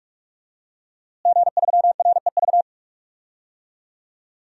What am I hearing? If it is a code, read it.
Morse code, heard as a keyed tone, about 46 wpm, 700 Hz. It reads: G3REV
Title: DXCC-100 Call Signs - 5 Characters - Repeated 3x - 45wpm